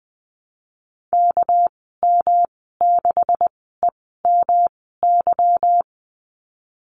Morse code, heard as a keyed tone, about 20 wpm, spaced as for 14 wpm, 700 Hz. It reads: KM6EMY
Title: 245 Lesson 39g - Letter z - Call Signs 20wpm